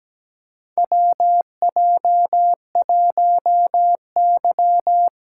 Morse code, heard as a keyed tone, about 17 wpm, 700 Hz. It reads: WJ1Y
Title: USA Callsign DE Callsign - 4 Characters - 17wpm